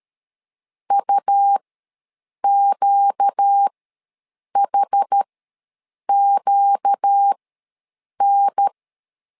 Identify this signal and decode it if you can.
Morse code, heard as a keyed tone, about 13 wpm, spaced as for 7 wpm, 800 Hz. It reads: UQHQN
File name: LEZIONI / 4 Q ws 12wpm 1912